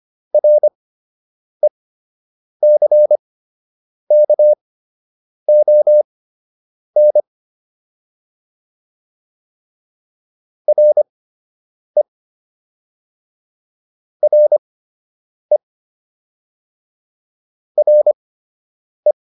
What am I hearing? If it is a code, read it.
Morse code, heard as a keyed tone, about 25 wpm, spaced as for 8 wpm, 600 Hz. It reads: RECKON RE RE RE